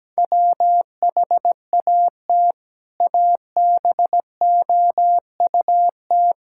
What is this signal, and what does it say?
Morse code, heard as a keyed tone, about 17 wpm, 700 Hz. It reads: WHAT ABOUT
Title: Sentences from Top 1600 Words 17wpm